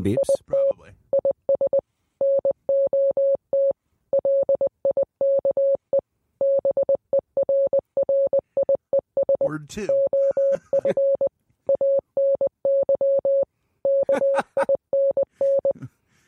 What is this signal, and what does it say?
Morse code, heard as a keyed tone, about 20 wpm, 550 Hz, with speech. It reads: IT IS NOT LIKE BERRIES OF ANY KINN